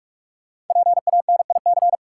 Morse code, heard as a keyed tone, about 46 wpm, 700 Hz. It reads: PANIC
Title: Five Letter Words - 45wpm